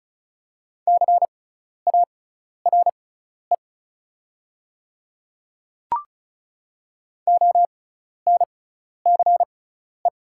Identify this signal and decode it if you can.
Morse code, heard as a keyed tone, about 34 wpm, spaced as for 12 wpm, 700 Hz. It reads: CARE ONCE
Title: 6x Character Spacing - Top 300 Words - 35wpm